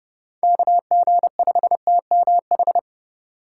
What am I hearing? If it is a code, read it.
Morse code, heard as a keyed tone, about 31 wpm, 700 Hz. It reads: KG5TMH